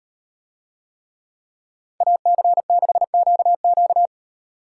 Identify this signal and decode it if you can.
Morse code, heard as a keyed tone, about 35 wpm, 700 Hz. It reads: AC6QQ